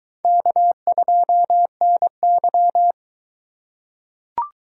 Morse code, heard as a keyed tone, about 21 wpm, 700 Hz. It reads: K2NY